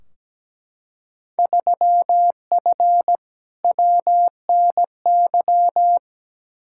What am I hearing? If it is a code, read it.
Morse code, heard as a keyed tone, about 17 wpm, 700 Hz. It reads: 3F WNY
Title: Field Day Exchanges - 17wpm